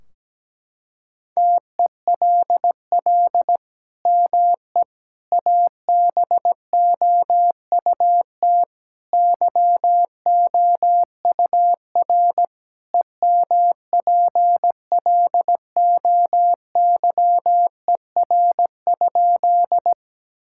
Morse code, heard as a keyed tone, about 17 wpm, 700 Hz. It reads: TELL ME ABOUT YOUR EMPLOYER?
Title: Sentences from Top 1600 Words 17wpm